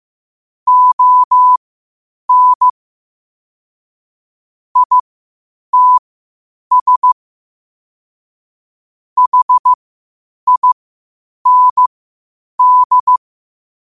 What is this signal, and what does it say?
Morse code, heard as a keyed tone, about 15 wpm, spaced as for 8 wpm, 1000 Hz. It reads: ON ITS HIND
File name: qotdfiles / 8wpm